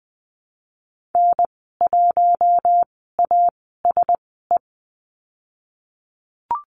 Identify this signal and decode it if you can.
Morse code, heard as a keyed tone, about 19 wpm, spaced as for 14 wpm, 700 Hz. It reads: N1ASE
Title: 048 Lesson 9g - Number 4 - Call Signs 20wpm